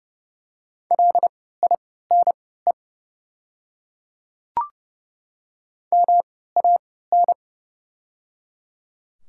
Morse code, heard as a keyed tone, about 30 wpm, spaced as for 17 wpm, 700 Hz. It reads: LINE MAN
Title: 3x Character Spacing - Top 200 Words - 30wpm